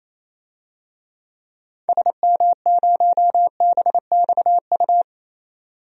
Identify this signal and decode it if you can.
Morse code, heard as a keyed tone, about 28 wpm, 700 Hz. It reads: SM0BXU
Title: DXCC-100 Call Signs - 6 Characters - 28wpm